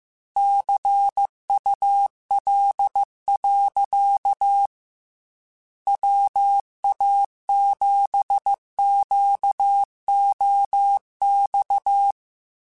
Morse code, heard as a keyed tone, about 15 wpm, 800 Hz. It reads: CUL. WA7QOX
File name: QSOs / QSOs-mp3s / WA7QOX 2@13